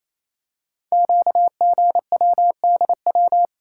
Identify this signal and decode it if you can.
Morse code, heard as a keyed tone, about 28 wpm, 700 Hz. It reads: QGWDW
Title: Code Groups - 5 Letters - 28wpm